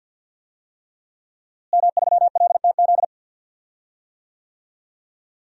Morse code, heard as a keyed tone, about 46 wpm, 700 Hz. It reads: M3LTZ
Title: DXCC-100 Call Signs - 5 Characters - Repeated 3x - 50wpm